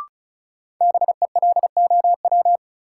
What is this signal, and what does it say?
Morse code, heard as a keyed tone, about 36 wpm, 700 Hz. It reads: BELOW
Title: Sentences from Top 1000 Words - Review Part 5 - New Word Drill - 35wpm